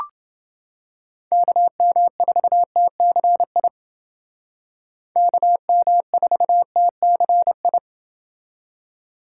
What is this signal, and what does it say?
Morse code, heard as a keyed tone, about 29 wpm, 700 Hz. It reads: KM4TCI KM4TCI